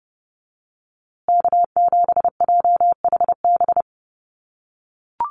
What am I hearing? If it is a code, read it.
Morse code, heard as a keyed tone, about 30 wpm, 700 Hz. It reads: K7JHB
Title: USA Call Signs Repeated 2x 30wpm